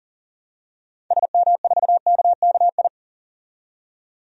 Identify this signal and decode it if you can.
Morse code, heard as a keyed tone, about 40 wpm, 700 Hz. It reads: SM4KKI
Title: DXCC-100 Call Signs - 6 Characters - 40wpm